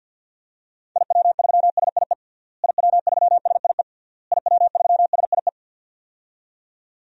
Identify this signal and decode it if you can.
Morse code, heard as a keyed tone, about 47 wpm, 700 Hz. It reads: IW3SIE IW3SIE IW3SIE
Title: DXCC-100 Call Signs - 6 Characters - Repeated 3x - 50wpm